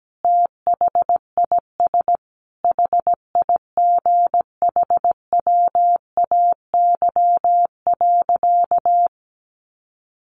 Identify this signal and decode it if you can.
Morse code, heard as a keyed tone, about 17 wpm, 700 Hz. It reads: THIS HIGHWAY.
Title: Sentences from Top 1400 Words 17wpm